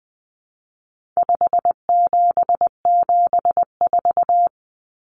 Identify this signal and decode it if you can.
Morse code, heard as a keyed tone, about 20 wpm, 700 Hz. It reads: 5774
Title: Common Cut Numbers - 4-Digit 20wpm